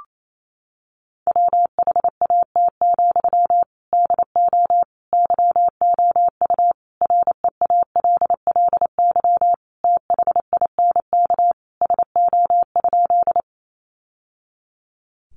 Morse code, heard as a keyed tone, about 28 wpm, 700 Hz. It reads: WHAT, DO YOU REALLY THINK SO?
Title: Sentences from Top 600 Words 28wpm